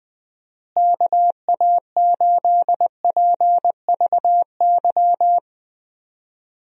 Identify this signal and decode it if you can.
Morse code, heard as a keyed tone, about 20 wpm, 700 Hz. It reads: KA8PVY
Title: USA Call Signs - 6 Characters - Repeated 3x - 20wpm